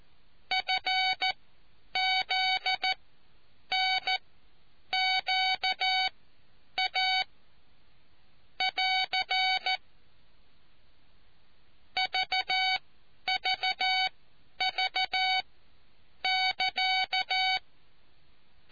Morse code, heard as a keyed tone, about 14 wpm, spaced as for 9 wpm, 750 Hz. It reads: FZNQA <AR> VVV <KA>